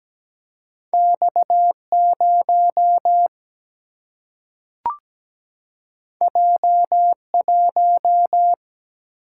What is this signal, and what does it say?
Morse code, heard as a keyed tone, about 17 wpm, 700 Hz. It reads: X0 J1
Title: Single-Letter Single-Number 17wpm